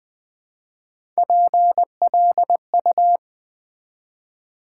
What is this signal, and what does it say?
Morse code, heard as a keyed tone, about 20 wpm, 700 Hz. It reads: PLU
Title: Build A Word - 3 Letters At A Time - 6-Letter Words - 20wpm